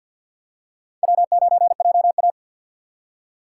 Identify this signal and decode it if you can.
Morse code, heard as a keyed tone, about 47 wpm, 700 Hz. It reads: W9JA
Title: USA Call Signs - 4 Characters - 2x Character Spacing - 50wpm